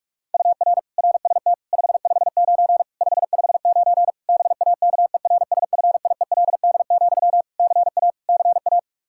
Morse code, heard as a keyed tone, about 43 wpm, 700 Hz. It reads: UR RST 559 559 BAKERSFIELD, CA CA
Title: K1USN SST QSO Practice 45wpm